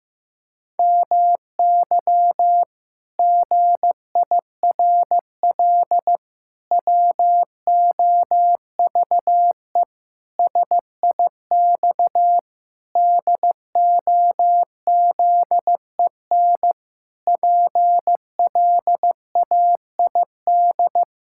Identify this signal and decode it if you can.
Morse code, heard as a keyed tone, about 14 wpm, 700 Hz. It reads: MY GIRL WOVE SIX DOZEN PLAID
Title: Pangram Challenge v2 - 15wpm